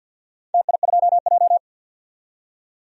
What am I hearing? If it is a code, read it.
Morse code, heard as a keyed tone, about 51 wpm, 700 Hz. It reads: TI2J
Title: DXCC-100 Call Signs - 4 Characters - 50wpm